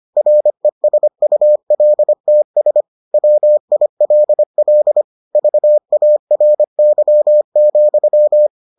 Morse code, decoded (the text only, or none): RESULTS WILL VARY,